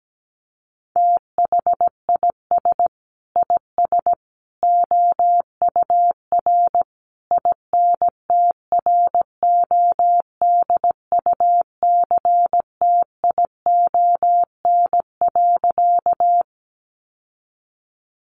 Morse code, heard as a keyed tone, about 17 wpm, 700 Hz. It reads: THIS IS OUR INTRODUCTION.
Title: Sentences from Top 1000 Words 17wpm